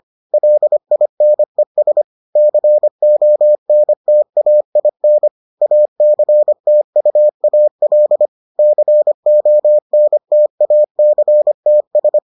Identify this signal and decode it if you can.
Morse code, heard as a keyed tone, about 25 wpm, 600 Hz. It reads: LINES CONTAIN ACTUAL CONTACTS